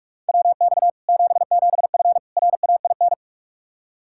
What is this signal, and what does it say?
Morse code, heard as a keyed tone, about 46 wpm, 700 Hz. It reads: WX 77F RAIN
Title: Practice QSO - Exchange 2 - Info - 45wpm